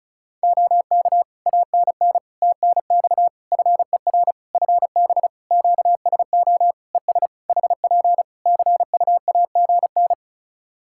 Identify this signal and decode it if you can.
Morse code, heard as a keyed tone, about 34 wpm, 700 Hz. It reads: OK ANN TNX FER FB QSO ES HP CUAGN
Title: K1USN SST QSO Practice 35wpm